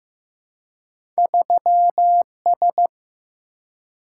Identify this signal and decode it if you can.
Morse code, heard as a keyed tone, about 15 wpm, 700 Hz. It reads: 3S
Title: Single-Letter Single-Number 15wpm